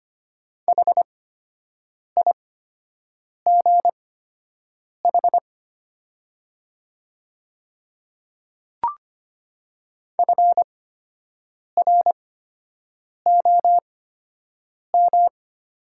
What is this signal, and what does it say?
Morse code, heard as a keyed tone, about 25 wpm, spaced as for 7 wpm, 700 Hz. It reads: HIGH FROM
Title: 8x Character Spacing - Top 200 Words - 25wpm